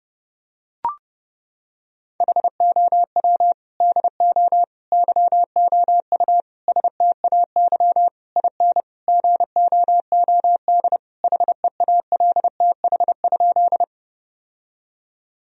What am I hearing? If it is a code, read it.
Morse code, heard as a keyed tone, about 30 wpm, 700 Hz. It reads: HOW DO YOU STAY IN GOOD HEALTH?
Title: Sentences from Top 1000 Words - Review Part 3 - 30wpm